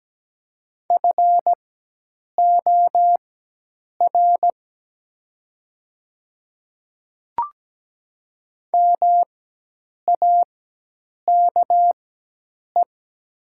Morse code, heard as a keyed tone, about 17 wpm, spaced as for 8 wpm, 700 Hz. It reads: FOR MAKE
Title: 4x Character Spacing - Top 100 Words - 17wpm